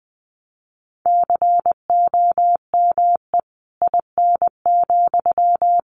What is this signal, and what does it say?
Morse code, heard as a keyed tone, about 20 wpm, 700 Hz. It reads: COME IN,
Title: Sentences from Top 1000 Words 20wpm